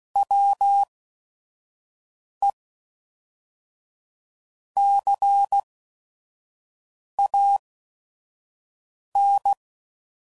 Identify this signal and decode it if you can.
Morse code, heard as a keyed tone, about 16 wpm, spaced as for 5 wpm, 800 Hz. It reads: WE CAN
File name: CW Code Groups @5WPM / QSOs / QSO WD0FBT